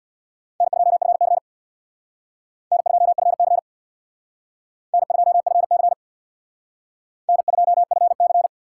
Morse code, heard as a keyed tone, about 62 wpm, 700 Hz. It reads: N2FC N2FC N2FC N2FC
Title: Speed Racing - USA Call Signs 50wpm